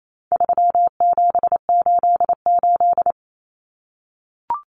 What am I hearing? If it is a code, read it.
Morse code, heard as a keyed tone, about 28 wpm, 700 Hz. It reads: 3788
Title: Common Cut Numbers - 4-Digit 28wpm